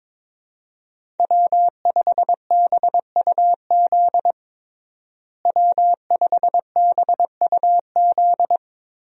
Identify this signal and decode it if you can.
Morse code, heard as a keyed tone, about 22 wpm, 700 Hz. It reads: W5BUZ W5BUZ
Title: USA Call Signs - 5 Characters - Repeated 2x - 22wpm